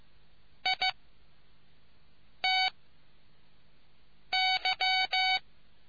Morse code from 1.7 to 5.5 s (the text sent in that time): T Y